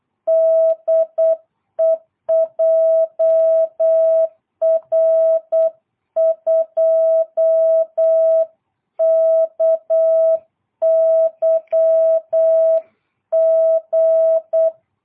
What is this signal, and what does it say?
Morse code, heard as a keyed tone, about 8 wpm, 660 Hz. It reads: DE JR2KYG
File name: jh8cbh / 2022cw